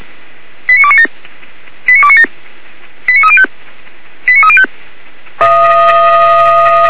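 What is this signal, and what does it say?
Five-tone selective call (ZVEI), heard as quick runs of five tones, tones of about 70 ms. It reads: 89297 89297 89305 89305
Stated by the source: Five-tone ZVEI selective call 89297, received by SDR.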